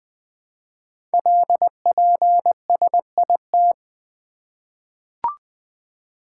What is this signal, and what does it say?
Morse code, heard as a keyed tone, about 20 wpm, 700 Hz. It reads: LPSIT
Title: Code Groups - 5 Letters - 20wpm